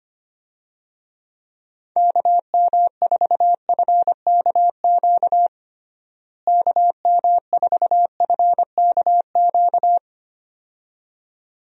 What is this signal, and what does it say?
Morse code, heard as a keyed tone, about 25 wpm, 700 Hz. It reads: KM4FKQ KM4FKQ
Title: USA Call Signs - 6 Characters - Repeated 2x - 25wpm